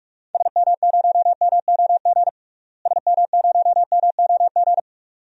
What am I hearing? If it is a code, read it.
Morse code, heard as a keyed tone, about 46 wpm, 700 Hz. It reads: SM0MOG SM0MOG
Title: DXCC-100 Call Signs - 6 Characters - Repeated 2x - 45wpm